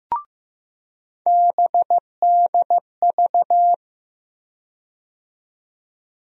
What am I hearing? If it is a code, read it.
Morse code, heard as a keyed tone, about 15 wpm, 700 Hz. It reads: BDV
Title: USA Call Sign Postfixes - 3 Letters 15wpm